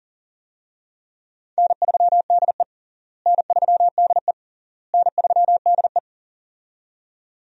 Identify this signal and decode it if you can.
Morse code, heard as a keyed tone, about 40 wpm, 700 Hz. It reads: N3DE N3DE N3DE
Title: USA Call Signs - 4 Characters - Repeated 3x - 40wpm